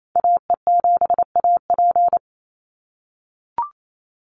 Morse code, heard as a keyed tone, about 28 wpm, 700 Hz. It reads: AE7AP
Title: NA SOTA Chaser Callsigns 28wpm